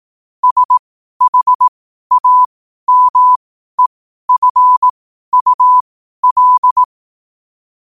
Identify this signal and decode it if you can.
Morse code, heard as a keyed tone, about 18 wpm, spaced as for 12 wpm, 1000 Hz. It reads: SHAMEFUL
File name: qotdfiles / 12wpm